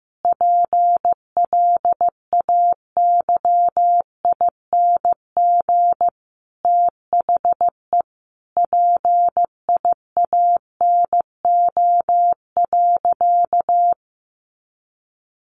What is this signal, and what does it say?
Morse code, heard as a keyed tone, about 15 wpm, 700 Hz. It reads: PLAYING THE PIANO.
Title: Additional Easy Phrases 15wpm